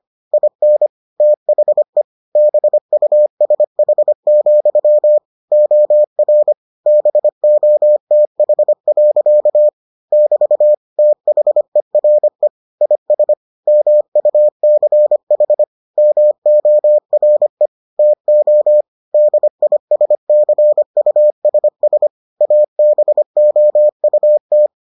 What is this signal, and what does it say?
Morse code, heard as a keyed tone, about 25 wpm, 600 Hz. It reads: IN THE BUSH, OR BOTH. <BT> THERE IS MUCH MORE TO DISCUSS ABOUT